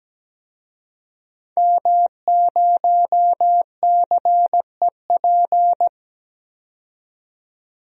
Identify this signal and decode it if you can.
Morse code, heard as a keyed tone, about 17 wpm, 700 Hz. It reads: M0CEP
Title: DXCC-100 Call Signs - 5 Characters - 17wpm